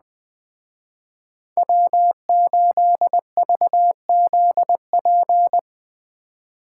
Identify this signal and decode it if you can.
Morse code, heard as a keyed tone, about 21 wpm, 700 Hz. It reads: W8VZP
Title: Speed Racing - USA 1x3 Call Signs 17wpm